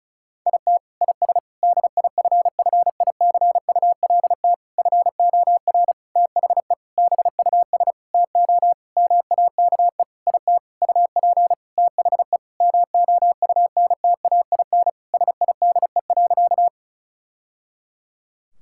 Morse code, heard as a keyed tone, about 34 wpm, 700 Hz. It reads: IT IS DIFFICULT FOR THE BUS TO MAKE IT UP THE MOUNTAIN SIDE.
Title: Sentences from Top 1000 Words - Review Part 6 - 35wpm